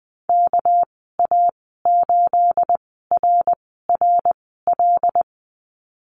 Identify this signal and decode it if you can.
Morse code, heard as a keyed tone, about 20 wpm, spaced as for 15 wpm, 700 Hz. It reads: KA8RRL
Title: 213 Lesson 34g - Number 8 - Call Signs 20wpm